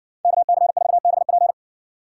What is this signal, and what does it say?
Morse code, heard as a keyed tone, about 60 wpm, 700 Hz. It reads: KC4BP